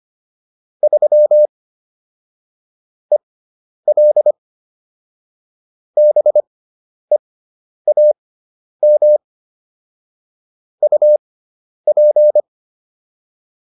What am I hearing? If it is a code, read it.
Morse code, heard as a keyed tone, about 25 wpm, spaced as for 10 wpm, 600 Hz. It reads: 3 EL BEAM UP
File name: QSO310 10